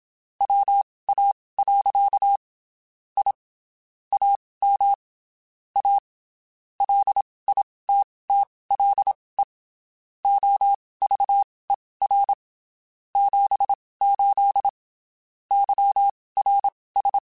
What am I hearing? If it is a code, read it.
Morse code, heard as a keyed tone, about 27 wpm, spaced as for 19 wpm, 800 Hz. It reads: WA. I AM A LITTLE OVER 78 YRS
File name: QSOs Spaced